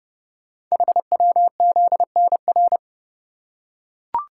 Morse code, heard as a keyed tone, about 30 wpm, 700 Hz. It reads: HWZNR
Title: Code Groups - 5 Letter:Numbers - 30wpm